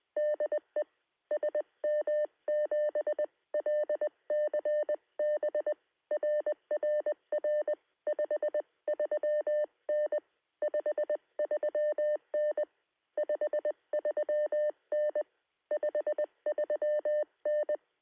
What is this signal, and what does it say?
Morse code, heard as a keyed tone, about 20 wpm, 600 Hz. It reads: DE SM7LCB RRR 53N 53N 53N 53N
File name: sm7lcb / 2117z g3xdy 9cm